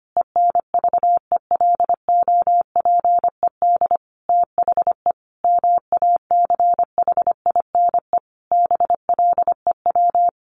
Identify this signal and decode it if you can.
Morse code, heard as a keyed tone, about 25 wpm, 700 Hz. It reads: ENVELOPED THE MACHINE BLEW
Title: A Sound Of Thunder 25wpm